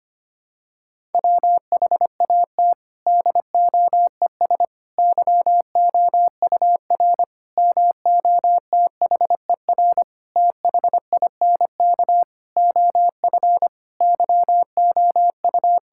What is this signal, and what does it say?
Morse code, heard as a keyed tone, about 25 wpm, 700 Hz. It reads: WHAT DOES YOUR MOTHER THINK OF YOU
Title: Sentences from Top 1000 Words - Review Part 5 - 25wpm